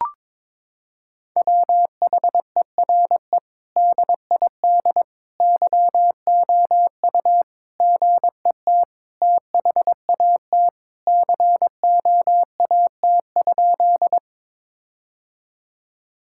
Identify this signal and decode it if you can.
Morse code, heard as a keyed tone, about 21 wpm, 700 Hz. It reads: WHERE DID YOU GET THAT COAT?
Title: Sentences from Top 1600 Words 22wpm